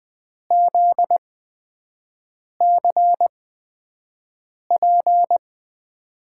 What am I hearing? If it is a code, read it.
Morse code, heard as a keyed tone, about 20 wpm, spaced as for 5 wpm, 700 Hz. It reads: ZCP
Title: USA Call Signs - 6 Characters - 8x Character Spacing - 20wpm